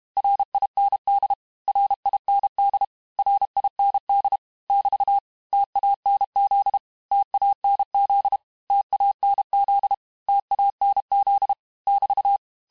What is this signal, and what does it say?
Morse code, heard as a keyed tone, about 32 wpm, 800 Hz. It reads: RIND RIND RIND <BT> TANZ TANZ TANZ TANZ <BT>
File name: cours-du-mercredi / audios / cours-868-teste-MOTS-vitesse-30mpm